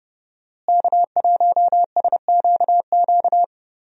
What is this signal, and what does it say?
Morse code, heard as a keyed tone, about 30 wpm, 700 Hz. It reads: K1SQQ
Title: USA Call Signs - 5 Characters - Repeated 3x - 30wpm